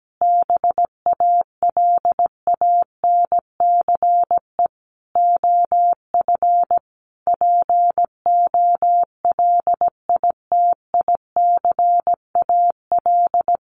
Morse code, heard as a keyed tone, about 17 wpm, 700 Hz. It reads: BALANCE OF POLITICAL
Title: Sentences from Top 900 Words 17wpm